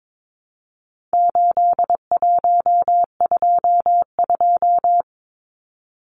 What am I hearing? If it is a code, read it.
Morse code, heard as a keyed tone, about 22 wpm, 700 Hz. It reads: 8122
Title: Numbers - 4-Digit 22wpm